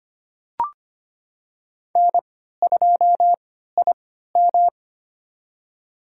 Morse code, heard as a keyed tone, about 25 wpm, spaced as for 14 wpm, 700 Hz. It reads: N2IM
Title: USA Call Signs - 3x Character Spacing - 25wpm